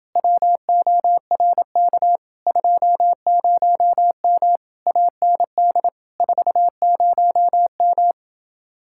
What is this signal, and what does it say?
Morse code, heard as a keyed tone, about 29 wpm, 700 Hz. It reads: WORK 20M AND 40M